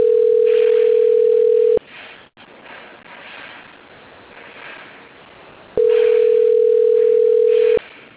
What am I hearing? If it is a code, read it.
Ringback tone (North America), 440 + 480 Hz.